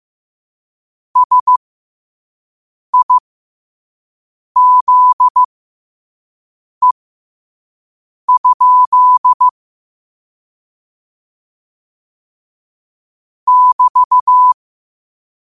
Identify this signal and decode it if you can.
Morse code, heard as a keyed tone, about 15 wpm, spaced as for 5 wpm, 1000 Hz. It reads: SIZE? <BT>